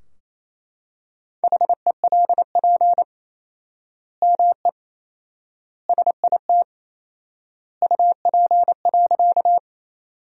Morse code, heard as a keyed tone, about 28 wpm, 700 Hz. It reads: HELP ME SIT UP.
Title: 4x Word-Spacing - Sentences from Top 500 Words - 28wpm